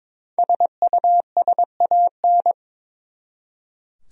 Morse code, heard as a keyed tone, about 22 wpm, 700 Hz. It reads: SUSAN
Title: USA Common Names - Max 5 Letters - 22wpm